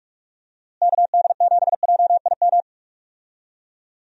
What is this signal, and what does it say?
Morse code, heard as a keyed tone, about 41 wpm, 700 Hz. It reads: KD7JIM
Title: USA Call Signs - 6 Characters - Repeated 3x - 45wpm